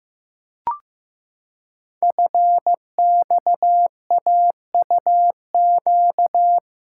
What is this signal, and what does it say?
Morse code, heard as a keyed tone, about 15 wpm, 700 Hz. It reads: FXAUQ
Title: Code Groups - 5 Letters - 15wpm